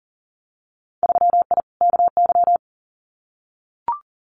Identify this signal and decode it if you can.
Morse code, heard as a keyed tone, about 38 wpm, 700 Hz. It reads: 3I KY
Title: Winter Field Day Exchanges 40wpm